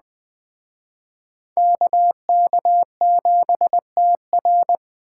Morse code, heard as a keyed tone, about 21 wpm, 700 Hz. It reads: KK7TR